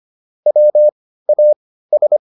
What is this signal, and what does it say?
Morse code, heard as a keyed tone, about 25 wpm, spaced as for 15 wpm, 600 Hz. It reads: WAS
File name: SS101 15